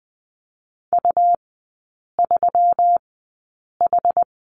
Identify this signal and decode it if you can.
Morse code, heard as a keyed tone, about 20 wpm, spaced as for 8 wpm, 700 Hz. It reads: U3H
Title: 121 Lesson 20e - Number 3 - Three Characters 20wpm